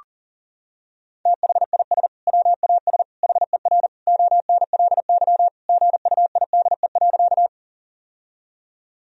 Morse code, heard as a keyed tone, about 40 wpm, 700 Hz. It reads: THIS WAS HER ONLY GUIDE.
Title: Sentences from Top 1000 Words - Review Part 3 - 40wpm